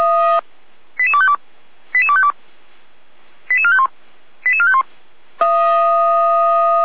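Five-tone selective call (ZVEI), heard as quick runs of five tones, tones of about 70 ms. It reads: 88252 88252 88451 88451
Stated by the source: ZVEI five-tone call received on SDR, feuerwehr-orlamuende.org fire brigade.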